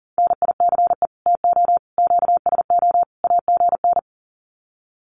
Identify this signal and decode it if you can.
Morse code, heard as a keyed tone, about 40 wpm, 700 Hz. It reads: NICE TO QSO AGN
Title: Common QSO Phrases 40wpm